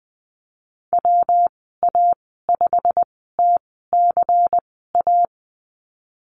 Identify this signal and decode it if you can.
Morse code, heard as a keyed tone, about 20 wpm, spaced as for 15 wpm, 700 Hz. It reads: WA5TCA
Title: 193 Lesson 31g - Letter v - Call Signs 20wpm